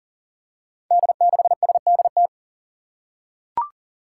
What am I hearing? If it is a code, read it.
Morse code, heard as a keyed tone, about 40 wpm, 700 Hz. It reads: D6SDT
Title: Five Challenging Letter-Number Combinations 40wpm